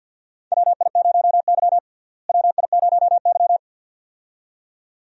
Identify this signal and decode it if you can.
Morse code, heard as a keyed tone, about 53 wpm, 700 Hz. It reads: WI0Y WI0Y
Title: USA Call Signs - 4 Characters - Repeated 2x - 50wpm